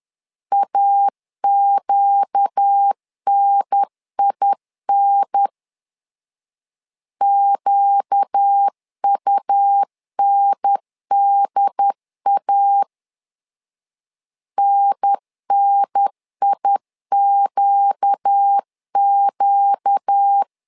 Morse code, heard as a keyed tone, about 10 wpm, 800 Hz. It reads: AQNIN QUNDA NNIQQ
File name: LEZIONI / 4 Q 10wpm 2210